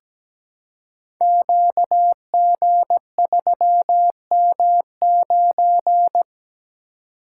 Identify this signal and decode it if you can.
Morse code, heard as a keyed tone, about 17 wpm, 700 Hz. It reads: QG3M9
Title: Code Groups - 5 Letter:Numbers - 17wpm